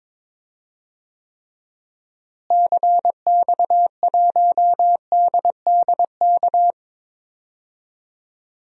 Morse code, heard as a keyed tone, about 22 wpm, 700 Hz. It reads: CX1DDK
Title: DXCC-100 Call Signs - 6 Characters - Repeated 3x - 22wpm